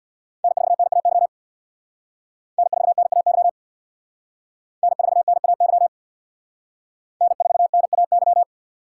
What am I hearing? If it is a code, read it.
Morse code, heard as a keyed tone, about 60 wpm, 700 Hz. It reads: N4NAY N4NAY N4NAY N4NAY
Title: Speed Racing - USA 1x3 Call Signs 50wpm